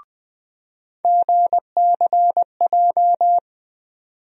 Morse code, heard as a keyed tone, about 20 wpm, 700 Hz. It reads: GCJ
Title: USA Call Sign Postfixes - 3 Letters 20wpm